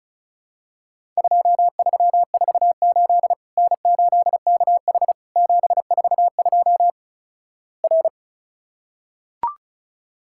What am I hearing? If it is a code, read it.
Morse code, heard as a keyed tone, about 36 wpm, 700 Hz. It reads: 2348 N8KH 742 R
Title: Worked All Europe DX Contest - QTC Exchange - 35wpm